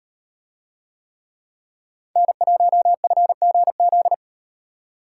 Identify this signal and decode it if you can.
Morse code, heard as a keyed tone, about 38 wpm, 700 Hz. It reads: N1FGZ